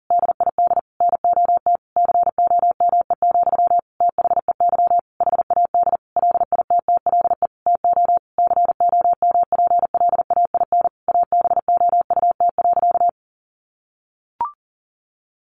Morse code, heard as a keyed tone, about 40 wpm, 700 Hz. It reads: DID NOT COME, THEY HAD LITTLE TO COMPLAIN ABOUT.
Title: Sentences from Top 1300 Words 40wpm